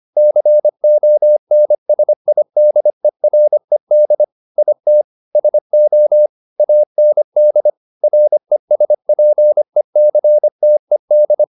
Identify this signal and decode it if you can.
Morse code, heard as a keyed tone, about 25 wpm, 600 Hz. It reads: CONSIDERED IT SO AND RESPECTED
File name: the old man and the sea part 1 25wpm